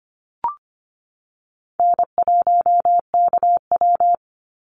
Morse code, heard as a keyed tone, about 25 wpm, 700 Hz. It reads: N1KW
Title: Easy Call Signs - 25wpm